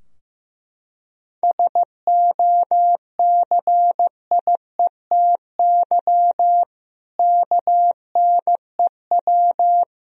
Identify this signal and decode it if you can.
Morse code, heard as a keyed tone, about 15 wpm, 700 Hz. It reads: SOCIETY KNEW
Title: Sentences from Top 1000 Words - Review Part 4 - 15wpm